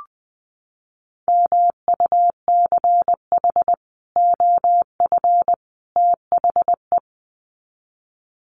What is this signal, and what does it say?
Morse code, heard as a keyed tone, about 20 wpm, 700 Hz. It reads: MUCH OF THE
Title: Sets of 3 Words 20wpm